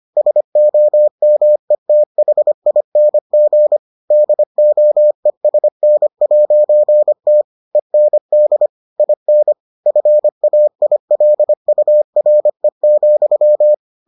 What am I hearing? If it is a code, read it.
Morse code, heard as a keyed tone, about 25 wpm, 600 Hz. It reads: SOMETHING DOESN'T END IN FAILURE,